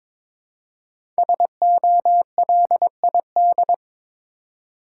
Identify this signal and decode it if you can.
Morse code, heard as a keyed tone, about 22 wpm, 700 Hz. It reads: SOLID
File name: 5 Letter Words - First 3 Letters Given - 22wpm